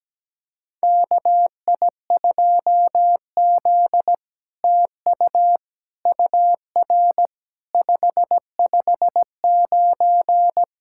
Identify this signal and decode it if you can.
Morse code, heard as a keyed tone, about 17 wpm, 700 Hz. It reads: KI2Z TU UR 559
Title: SOTA QSO Practice 17wpm